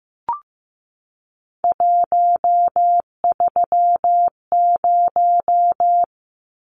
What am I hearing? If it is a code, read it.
Morse code, heard as a keyed tone, about 15 wpm, 700 Hz. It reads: 130
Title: Numbers - 3-Digit 15wpm